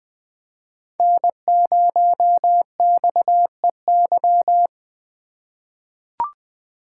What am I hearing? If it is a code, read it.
Morse code, heard as a keyed tone, about 20 wpm, 700 Hz. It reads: N0XEY
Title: USA Call Signs - 5 Characters - 3x Character Spacing - 20wpm